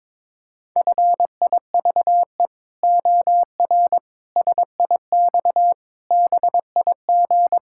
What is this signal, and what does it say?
Morse code, heard as a keyed tone, about 21 wpm, 700 Hz. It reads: FIVE OR SIX BIG
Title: Pangram Challenge v2 - 22wpm